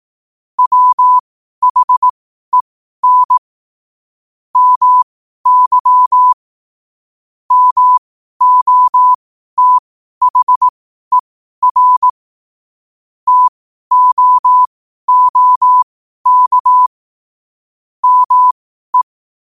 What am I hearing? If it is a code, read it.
Morse code, heard as a keyed tone, about 18 wpm, spaced as for 12 wpm, 1000 Hz. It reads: WHEN MY MOTHER TOOK ME